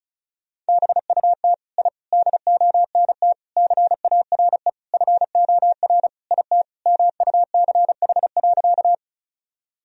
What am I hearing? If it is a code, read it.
Morse code, heard as a keyed tone, about 34 wpm, 700 Hz. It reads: BUT I DONT CARE FOR IT MUCH.
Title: Sentences from Top 1000 Words - Review Part 1 - 35wpm